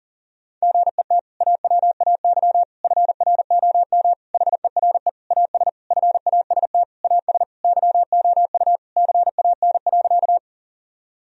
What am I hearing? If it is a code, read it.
Morse code, heard as a keyed tone, about 40 wpm, 700 Hz. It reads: GET AWAY FROM HERE AS FAST AS YOU CAN.
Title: Sentences from Top 1000 Words - Review Part 2 - 40wpm